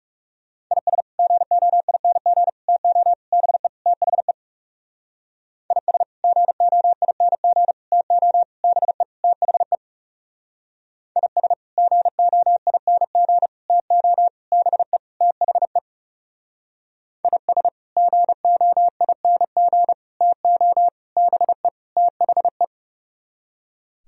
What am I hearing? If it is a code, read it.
Morse code, heard as a keyed tone, about 37 wpm, 700 Hz. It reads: IS GOING TO BE THE IS GOING TO BE THE IS GOING TO BE THE IS GOING TO BE THE
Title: Speed-Racing - Sets of 5 Words 30wpm